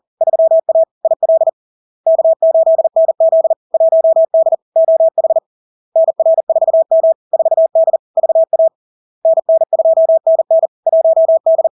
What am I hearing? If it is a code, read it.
Morse code, heard as a keyed tone, about 40 wpm, 650 Hz. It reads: 3A IL K8NZ 1D OH NR4M 4D VA NN2NN 1D
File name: LARK-FD-2022-Log-in-CW-40wpm-650-Hz